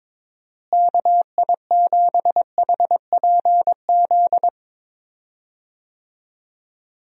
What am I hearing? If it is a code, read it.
Morse code, heard as a keyed tone, about 22 wpm, 700 Hz. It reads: KI7HPZ
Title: USA Call Signs - 6 Characters - Repeated 2x - 22wpm